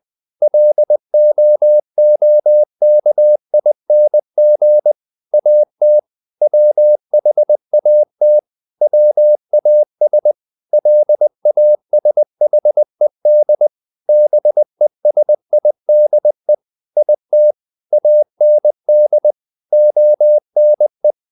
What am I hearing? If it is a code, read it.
Morse code, heard as a keyed tone, about 20 wpm, 600 Hz. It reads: LOOKING AT WHAT WAS LASHED BESIDE IT AND ONE
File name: the old man and the sea part 2 20wpm